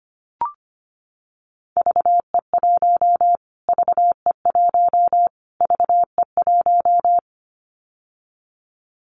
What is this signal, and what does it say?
Morse code, heard as a keyed tone, about 25 wpm, 700 Hz. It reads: VE1 VE1 VE1
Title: Callsign Prefixes for Top 25 DXCC - Repeated 3x - 25wpm